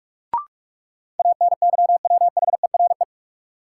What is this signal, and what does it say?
Morse code, heard as a keyed tone, about 46 wpm, 700 Hz. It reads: ANYWHERE
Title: Sentences from Top 1400 Words - New Word Drill - 45wpm